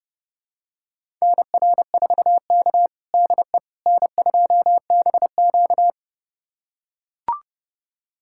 Morse code, heard as a keyed tone, about 32 wpm, 700 Hz. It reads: NR4K DE N2BQ
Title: USA Callsign DE Callsign - 4 Characters - 30wpm